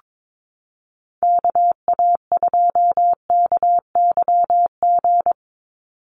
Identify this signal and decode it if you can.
Morse code, heard as a keyed tone, about 22 wpm, 700 Hz. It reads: KA2KYG
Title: USA Call Signs 22wpm